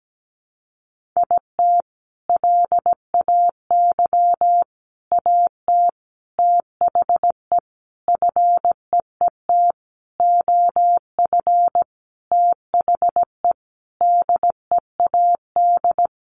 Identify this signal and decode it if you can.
Morse code, heard as a keyed tone, about 17 wpm, 700 Hz. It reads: IT LAY AT THE FEET OF THE DEAD
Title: Sentences from Top 1500 Words 17wpm